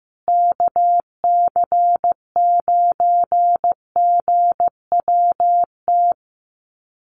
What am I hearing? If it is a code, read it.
Morse code, heard as a keyed tone, about 15 wpm, 700 Hz. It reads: KC9GWT
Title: USA Call Signs Repeated 3x 15wpm